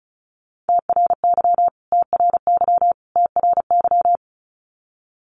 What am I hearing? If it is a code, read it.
Morse code, heard as a keyed tone, about 34 wpm, 700 Hz. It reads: TRY TRY TRY
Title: Top 500 Words Repeated 35wpm